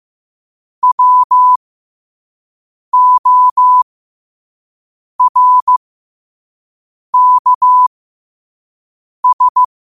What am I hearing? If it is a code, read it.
Morse code, heard as a keyed tone, about 15 wpm, spaced as for 5 wpm, 1000 Hz. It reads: WORKS
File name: qotdfiles / 5wpm